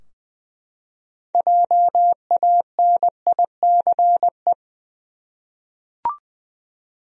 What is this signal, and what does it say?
Morse code, heard as a keyed tone, about 21 wpm, 700 Hz. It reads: JANICE
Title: USA Common Names - 20wpm